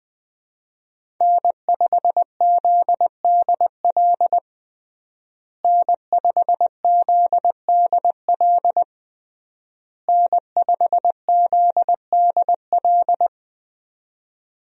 Morse code, heard as a keyed tone, about 20 wpm, 700 Hz. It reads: N5ZDL N5ZDL N5ZDL
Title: USA Call Signs - 5 Characters - Repeated 3x - 20wpm